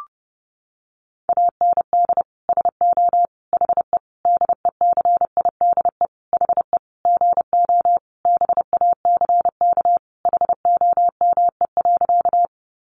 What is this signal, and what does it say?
Morse code, heard as a keyed tone, about 30 wpm, 700 Hz. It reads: AND SO HE DECIDE HE GO BACK HOME.
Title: Sentences from Top 500 Words 30wpm